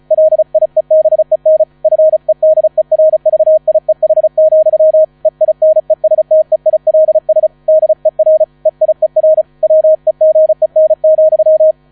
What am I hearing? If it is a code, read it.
Morse code, heard as a keyed tone, about 36 wpm, 630 Hz. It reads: LIEBEN FEDERVIEH, EINESTEILS DER EIER WEGEN,
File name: 35wpm